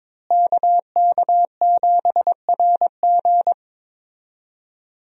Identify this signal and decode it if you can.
Morse code, heard as a keyed tone, about 22 wpm, 700 Hz. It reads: KK7RG
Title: USA Call Signs - 5 Characters - Repeated 3x - 22wpm